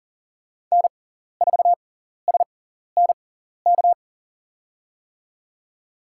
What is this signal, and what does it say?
Morse code, heard as a keyed tone, about 40 wpm, spaced as for 14 wpm, 700 Hz. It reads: N4SNK
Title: USA Call Signs - 5 Characters - 6x Character Spacing - 40wpm